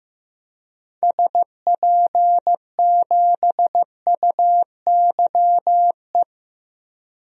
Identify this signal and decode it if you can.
Morse code, heard as a keyed tone, about 15 wpm, 690 Hz. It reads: SP7UYE
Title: DXCC-100 Call Signs - 6 Characters - 15wpm